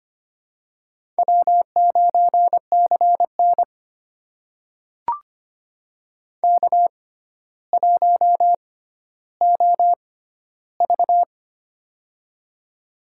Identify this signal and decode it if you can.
Morse code, heard as a keyed tone, about 25 wpm, spaced as for 13 wpm, 700 Hz. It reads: W9CN K1OV
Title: USA Call Signs - 6x Character Spacing - 25wpm